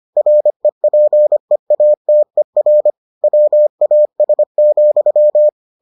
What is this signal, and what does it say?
Morse code, heard as a keyed tone, about 25 wpm, 600 Hz. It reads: REPEATER WAS,